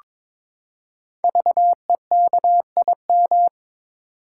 Morse code, heard as a keyed tone, about 22 wpm, 700 Hz. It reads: VEKIM
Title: Code Groups - 5 Letters - 22wpm